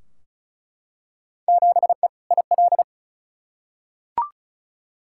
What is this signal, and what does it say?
Morse code, heard as a keyed tone, about 36 wpm, 700 Hz. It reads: 7E IL
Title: Field Day Exchanges Encore - 35wpm